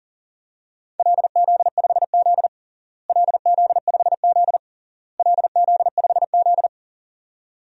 Morse code, heard as a keyed tone, about 37 wpm, 700 Hz. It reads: LZ5Z LZ5Z LZ5Z
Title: DXCC-100 Call Signs - 4 Characters - Repeated 3x - 40wpm